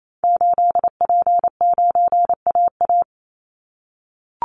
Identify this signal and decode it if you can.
Morse code, heard as a keyed tone, about 28 wpm, 700 Hz. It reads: 8P9AA
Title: Call Signs - 28wpm